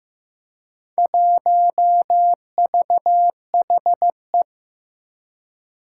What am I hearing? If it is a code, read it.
Morse code, heard as a keyed tone, about 15 wpm, 700 Hz. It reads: 1VHE
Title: Four Challenging Letter-Number Combinations 15wpm